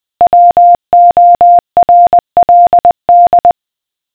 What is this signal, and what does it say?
Morse code, heard as a keyed tone, about 20 wpm, 700 Hz. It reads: WORLD
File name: examples / morse ref